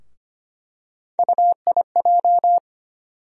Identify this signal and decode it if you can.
Morse code, heard as a keyed tone, about 25 wpm, 700 Hz. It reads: UIJ
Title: USA Call Sign Postfixes - 3 Letters 25wpm